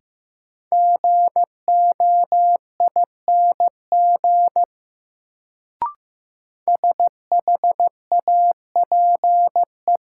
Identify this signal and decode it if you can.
Morse code, heard as a keyed tone, about 15 wpm, 700 Hz. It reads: GOING SHAPE
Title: Sentences from Top 500 Words - New Word Drill - 15wpm